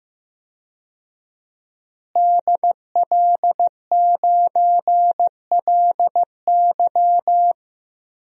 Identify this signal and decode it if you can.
Morse code, heard as a keyed tone, about 15 wpm, 700 Hz. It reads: DL9LY
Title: DXCC-100 Call Signs - 5 Characters - Repeated 2x - 15wpm